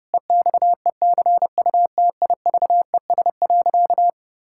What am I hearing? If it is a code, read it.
Morse code, heard as a keyed tone, about 32 wpm, 700 Hz. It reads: EXECUTIVES.
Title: Pangram Challenge v2 - 30wpm